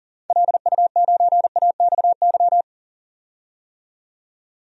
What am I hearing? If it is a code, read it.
Morse code, heard as a keyed tone, about 38 wpm, 700 Hz. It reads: LU9AXY
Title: DXCC-100 Call Signs - 6 Characters - 40wpm